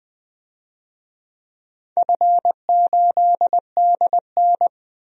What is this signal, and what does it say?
Morse code, heard as a keyed tone, about 20 wpm, 700 Hz. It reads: F8DN
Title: DXCC-100 Call Signs - 4 Characters - 20wpm